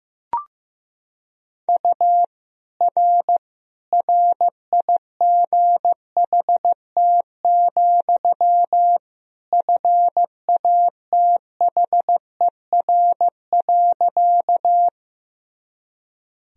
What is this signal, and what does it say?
Morse code, heard as a keyed tone, about 15 wpm, 700 Hz. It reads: U R RIGHT, FATHER.
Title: Sentences from Top 300 Words and Abbreviations 15wpm